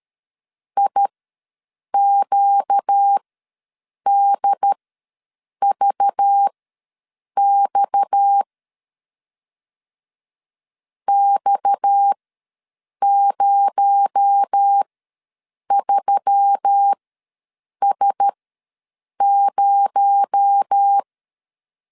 Morse code, heard as a keyed tone, about 13 wpm, spaced as for 7 wpm, 800 Hz. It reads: IQDVX X03S0
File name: LEZIONI / 6 PKX0 / 6 PKX0 ws 12wpm 3448